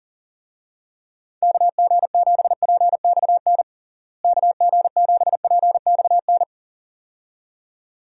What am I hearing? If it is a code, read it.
Morse code, heard as a keyed tone, about 40 wpm, 700 Hz. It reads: KG7PXN KG7PXN
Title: USA Call Signs - 6 Characters - Repeated 2x - 40wpm